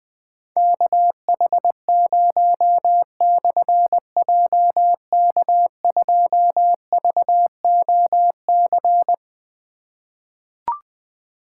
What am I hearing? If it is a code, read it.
Morse code, heard as a keyed tone, about 20 wpm, 700 Hz. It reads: KH0/JK2VOC
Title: Difficult Call Signs 20wpm